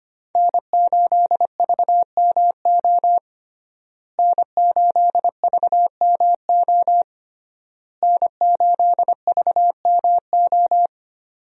Morse code, heard as a keyed tone, about 25 wpm, 700 Hz. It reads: N8VMO N8VMO N8VMO
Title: USA Call Signs - 5 Characters - Repeated 3x - 25wpm